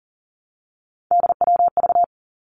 Morse code, heard as a keyed tone, about 40 wpm, 700 Hz. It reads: DWV
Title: Three Challenging Letter-Number Combinations 40wpm